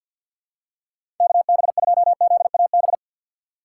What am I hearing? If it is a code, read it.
Morse code, heard as a keyed tone, about 48 wpm, 700 Hz. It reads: KB2ZAB
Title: USA Call Signs - 6 Characters - 3x Character Spacing - 50wpm